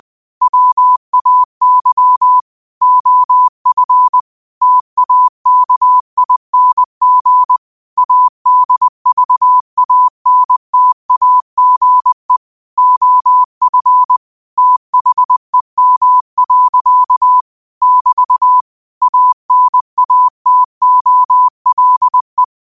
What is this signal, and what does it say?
Morse code, heard as a keyed tone, about 20 wpm, 1000 Hz. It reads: WAY OF TAKING ADVANTAGE OF THEM. <BT> ANATOLE
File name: qotdfiles / 20wpm